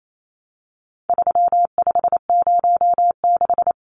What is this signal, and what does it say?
Morse code, heard as a keyed tone, about 28 wpm, 700 Hz. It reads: 3506
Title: Numbers - 4-Digit 28wpm